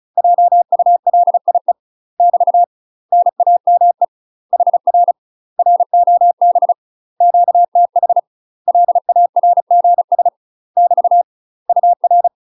Morse code, heard as a keyed tone, about 36 wpm, 700 Hz. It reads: JULIE <BT> NAME HR ROB QTH LARGS <BT> UR
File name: UK MASTER QSOs TRAINING / 35WPM / UK MASTER QSOs 0011